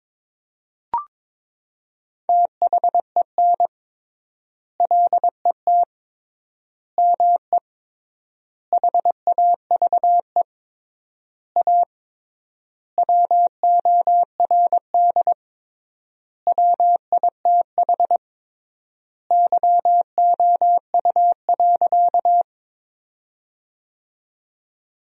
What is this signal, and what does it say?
Morse code, heard as a keyed tone, about 22 wpm, 700 Hz. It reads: THEN LET ME HAVE A WORD WITH YOU.
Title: 3x Word-Spacing - Sentences from Top 300 Words - 22wpm